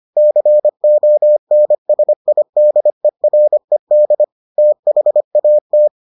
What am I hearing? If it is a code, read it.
Morse code, heard as a keyed tone, about 25 wpm, 600 Hz. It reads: CONSIDERED THAT